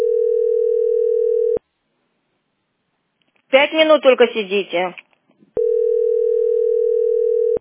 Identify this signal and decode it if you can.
Ringback tone (North America), 440 + 480 Hz, with speech.
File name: c61bfc0d-7b87-4d0f-8f73